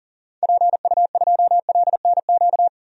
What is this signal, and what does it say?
Morse code, heard as a keyed tone, about 40 wpm, 700 Hz. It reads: PU2LNQ
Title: DXCC-100 Call Signs - 6 Characters - Repeated 2x - 40wpm